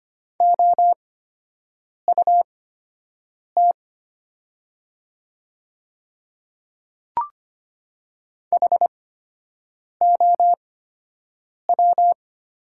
Morse code, heard as a keyed tone, about 25 wpm, spaced as for 7 wpm, 700 Hz. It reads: OUT HOW
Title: 8x Character Spacing - Top 100 Words - 25wpm